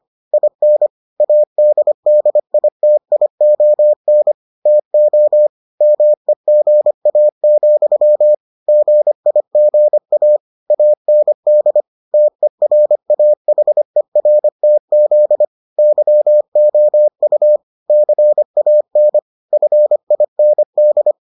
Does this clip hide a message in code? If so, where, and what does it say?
Yes: Morse code from 0.3 to 21.1 s, heard as a keyed tone, about 25 wpm, 600 Hz. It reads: IN ADDITION TO MEGA, GIGA AND TERAHERTZ YOU CAN FIND